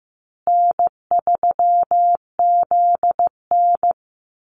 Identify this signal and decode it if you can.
Morse code, heard as a keyed tone, about 15 wpm, 700 Hz. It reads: N3ZN
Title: Easy Call Signs Repeated 3x - 15wpm